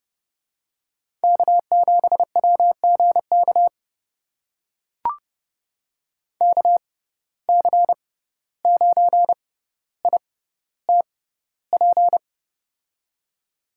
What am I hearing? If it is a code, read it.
Morse code, heard as a keyed tone, about 30 wpm, spaced as for 15 wpm, 700 Hz. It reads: K7WGK KC9ITP
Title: USA Call Signs - 6x Character Spacing - 30wpm